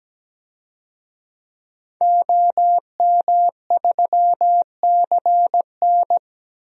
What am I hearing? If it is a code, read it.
Morse code, heard as a keyed tone, about 17 wpm, 700 Hz. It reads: OM3CN
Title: DXCC-100 Call Signs - 5 Characters - 17wpm